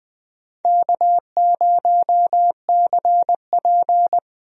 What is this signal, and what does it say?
Morse code, heard as a keyed tone, about 20 wpm, 690 Hz. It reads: K0CP
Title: USA Call Signs - 4 Characters - Repeated 3x - 20wpm